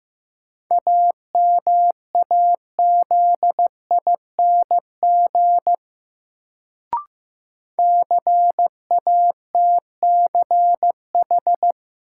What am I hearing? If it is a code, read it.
Morse code, heard as a keyed tone, about 15 wpm, 700 Hz. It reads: AMAZING CATCH
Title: Sentences from Top 1500 Words - New Word Drill Alt - 15wpm